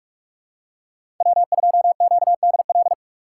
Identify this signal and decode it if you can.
Morse code, heard as a keyed tone, about 43 wpm, 700 Hz. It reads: W2QDL
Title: USA Call Signs - 3x Character Spacing - 45wpm